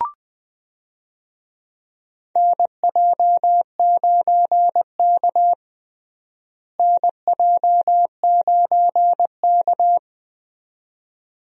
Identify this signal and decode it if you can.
Morse code, heard as a keyed tone, about 20 wpm, 700 Hz. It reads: NJ9K NJ9K
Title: USA Call Signs - 4 Characters - Repeated 2x - 20wpm